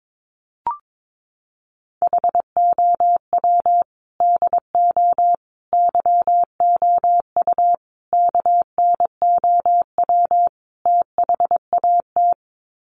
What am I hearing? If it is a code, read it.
Morse code, heard as a keyed tone, about 22 wpm, 700 Hz. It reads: HOW DO YOU KNOW THAT
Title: Sets of 5 Words Encore 22wpm